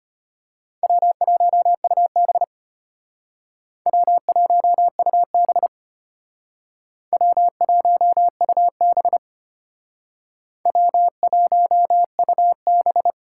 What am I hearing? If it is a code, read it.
Morse code, heard as a keyed tone, about 32 wpm, 700 Hz. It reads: W1UB W1UB W1UB W1UB